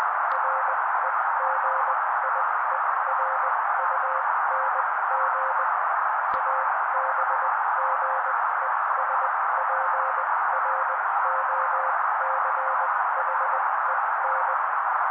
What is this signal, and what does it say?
Morse code, heard as a keyed tone, about 20 wpm, spaced as for 16 wpm, 530 Hz. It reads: REGIERUNG ABGESPROCHEN
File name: daily / 16wpm noise